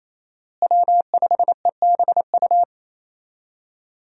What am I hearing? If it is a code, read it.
Morse code, heard as a keyed tone, about 28 wpm, 700 Hz. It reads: W5EBU
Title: USA Call Signs - 5 Characters - Repeated 2x - 28wpm